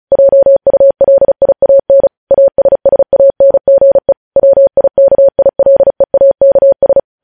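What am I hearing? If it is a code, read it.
Morse code, heard as a keyed tone, about 36 wpm, 550 Hz. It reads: JULIAN ASSANGE WIKILEAKS
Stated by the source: Morse news headlines sent at 35.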